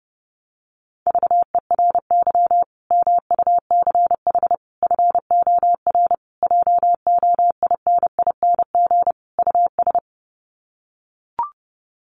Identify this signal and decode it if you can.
Morse code, heard as a keyed tone, about 30 wpm, 700 Hz. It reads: VERY MUCH FOR JOINING US
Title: Sets of 5 Words 30wpm